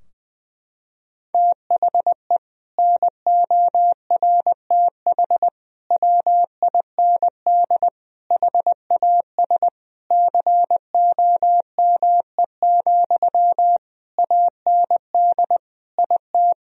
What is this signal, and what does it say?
Morse code, heard as a keyed tone, about 20 wpm, 700 Hz. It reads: THE NORTH WIND HAS COME, AND IT
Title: Sentences from Top 1000 Words - Review Part 3 - 20wpm